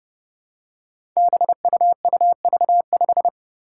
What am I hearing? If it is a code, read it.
Morse code, heard as a keyed tone, about 32 wpm, 700 Hz. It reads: BUUV5
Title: Five Challenging Letter-Number Combinations 30wpm